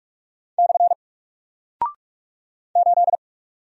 Morse code, heard as a keyed tone, about 44 wpm, 700 Hz. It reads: / 8
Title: Single Character (Letter:Number:Punctuation:Prosign) Alt 45wpm